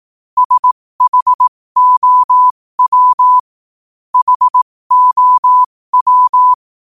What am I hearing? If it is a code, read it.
Morse code, heard as a keyed tone, about 18 wpm, spaced as for 15 wpm, 1000 Hz. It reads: SHOW HOW
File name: qotdfiles / 15wpm